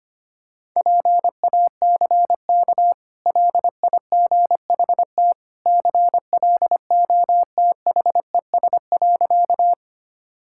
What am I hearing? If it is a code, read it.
Morse code, heard as a keyed tone, about 25 wpm, 690 Hz. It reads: PACK LIGHT CLOTHES.
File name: Easy Sentences with Hints - 25wpm